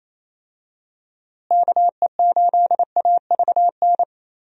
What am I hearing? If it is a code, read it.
Morse code, heard as a keyed tone, about 28 wpm, 700 Hz. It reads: KE8AVN
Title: USA Call Signs - 6 Characters - Repeated 3x - 28wpm